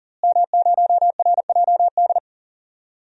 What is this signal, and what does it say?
Morse code, heard as a keyed tone, about 40 wpm, 700 Hz. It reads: M0RJD
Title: DXCC-100 Call Signs - 5 Characters - Repeated 3x - 40wpm